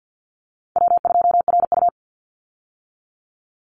Morse code, heard as a keyed tone, about 50 wpm, 700 Hz. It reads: W2RU
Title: Easy Call Signs - 50wpm